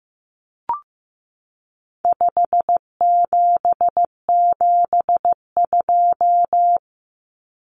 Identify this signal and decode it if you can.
Morse code, heard as a keyed tone, about 15 wpm, 700 Hz. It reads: 5772
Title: Common Cut Numbers - 4-Digit 15wpm